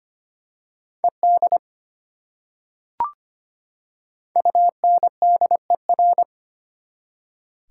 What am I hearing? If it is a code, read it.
Morse code, heard as a keyed tone, about 25 wpm, 700 Hz. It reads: ED UNDER
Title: Prefixes and Suffixes 25wpm